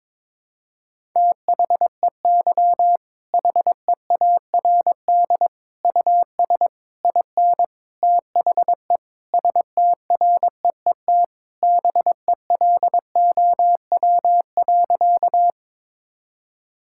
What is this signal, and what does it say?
Morse code, heard as a keyed tone, about 22 wpm, 700 Hz. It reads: THEY HEARD US IN THE STREET BELOW.
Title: Sentences from Top 1000 Words - Review Part 5 - 22wpm